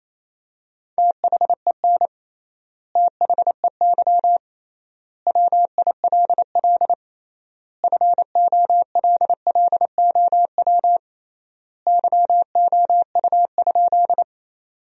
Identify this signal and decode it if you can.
Morse code, heard as a keyed tone, about 28 wpm, 700 Hz. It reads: THEN THEY WILL FOLLOW YOU?
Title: 3x Word-Spacing - Sentences from Top 200 Words - 28wpm